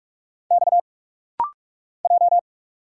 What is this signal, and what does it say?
Morse code, heard as a keyed tone, about 46 wpm, 700 Hz. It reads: X J
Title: Single Letters - Rapid-Fire 45wpm